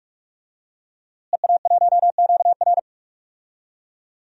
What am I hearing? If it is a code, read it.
Morse code, heard as a keyed tone, about 41 wpm, 700 Hz. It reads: EA1QR
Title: DXCC-100 Call Signs - 5 Characters - Repeated 2x - 45wpm